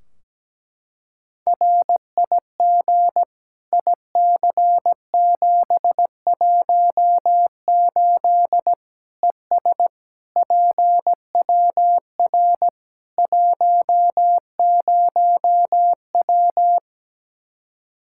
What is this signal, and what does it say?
Morse code, heard as a keyed tone, about 17 wpm, 700 Hz. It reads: RIG IC718 ES PWR 10W
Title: Practice QSO - Exchange 2 - Info - 17wpm